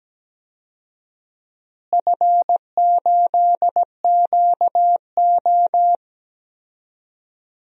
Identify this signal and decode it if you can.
Morse code, heard as a keyed tone, about 17 wpm, 700 Hz. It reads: F8QO
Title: DXCC-100 Call Signs - 4 Characters - Repeated 3x - 17wpm